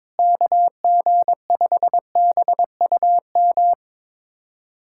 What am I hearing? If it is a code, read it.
Morse code, heard as a keyed tone, about 22 wpm, 700 Hz. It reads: KG5BUM
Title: USA Callsign DE Callsign - 6 Characters - 22wpm